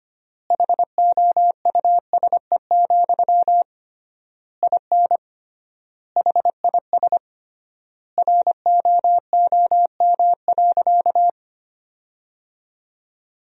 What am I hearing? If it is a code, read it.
Morse code, heard as a keyed tone, about 25 wpm, 700 Hz. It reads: HOUSE, IN HIS ROOM.
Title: 3x Word-Spacing - Sentences from Top 300 Words - 25wpm